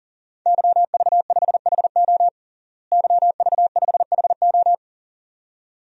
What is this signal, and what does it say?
Morse code, heard as a keyed tone, about 37 wpm, 700 Hz. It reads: YV5HO YV5HO
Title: DXCC-100 Call Signs - 5 Characters - Repeated 2x - 40wpm